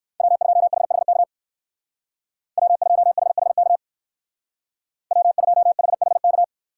Morse code, heard as a keyed tone, about 54 wpm, 700 Hz. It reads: W2HHK W2HHK W2HHK
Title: Speed Racing - USA 1x3 Call Signs 45wpm